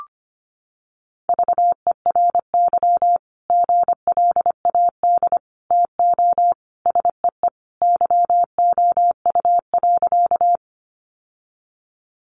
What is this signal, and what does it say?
Morse code, heard as a keyed tone, about 25 wpm, 700 Hz. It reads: VERY GLAD TO SEE YOU.
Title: Sentences from Top 800 Words 25wpm